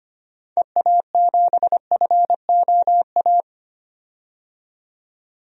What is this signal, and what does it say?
Morse code, heard as a keyed tone, about 25 wpm, 700 Hz. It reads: EA7FOA
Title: DXCC-100 Call Signs - 6 Characters - 25wpm